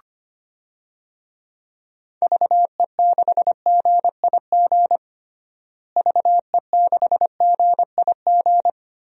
Morse code, heard as a keyed tone, about 25 wpm, 700 Hz. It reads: VE6GIG VE6GIG
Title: DXCC-100 Call Signs - 6 Characters - Repeated 3x - 25wpm